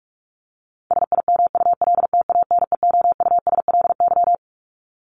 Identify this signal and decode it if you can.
Morse code, heard as a keyed tone, about 44 wpm, 700 Hz. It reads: SIMULTANEOUSLY
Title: LY Suffix - Top 100 Words with Base Words 45wpm